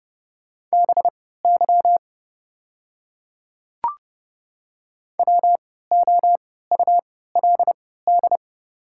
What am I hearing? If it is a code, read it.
Morse code, heard as a keyed tone, about 29 wpm, spaced as for 17 wpm, 700 Hz. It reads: BY WOULD
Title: 3x Character Spacing - Top 100 Words - 30wpm